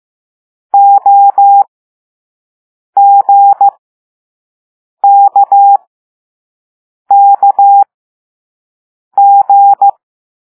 Morse code, heard as a keyed tone, about 15 wpm, spaced as for 5 wpm, 800 Hz. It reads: OGKKG
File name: Morse / Level4B